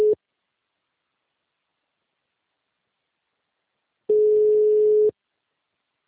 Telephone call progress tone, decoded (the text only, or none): ringback tone